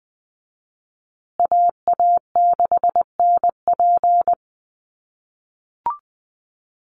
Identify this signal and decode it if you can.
Morse code, heard as a keyed tone, about 20 wpm, 700 Hz. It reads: AA6NP
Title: USA Call Signs Repeated 3x 20wpm